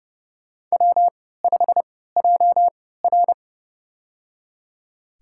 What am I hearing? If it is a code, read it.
Morse code, heard as a keyed tone, about 31 wpm, spaced as for 17 wpm, 700 Hz. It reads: W5JR
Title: USA Call Signs - 4 Characters - 3x Character Spacing - 30wpm